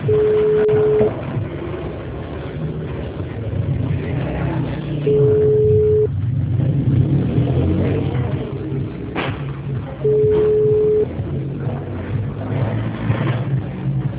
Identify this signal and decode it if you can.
Ringback tone (Europe, 425 Hz), with noise.